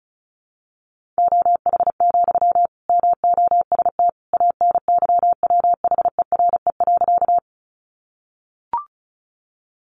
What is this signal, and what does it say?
Morse code, heard as a keyed tone, about 34 wpm, 700 Hz. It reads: OH, MOST ANYWHERE.
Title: Sentences from Top 1400 Words 35wpm